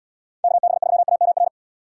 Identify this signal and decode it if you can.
Morse code, heard as a keyed tone, about 77 wpm, 700 Hz. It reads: KB3AMR